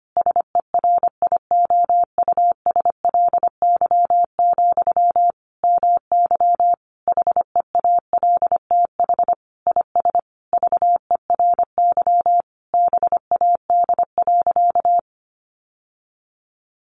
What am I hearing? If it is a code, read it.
Morse code, heard as a keyed tone, about 25 wpm, 700 Hz. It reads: SERIOUSLY, MY HEALTH IS VERY BAD.
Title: Sentences from Top 1200 Words 25wpm